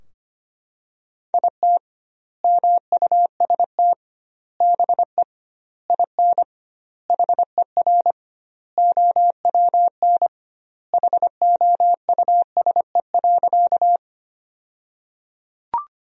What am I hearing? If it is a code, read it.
Morse code, heard as a keyed tone, about 25 wpm, 700 Hz. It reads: IT MUST BE IN HER OWN HOUSE.
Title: 2x Word-Spacing - Sentences from Top 300 Words - 25wpm